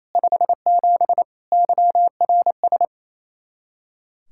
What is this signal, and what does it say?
Morse code, heard as a keyed tone, about 36 wpm, 700 Hz. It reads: 57 YRS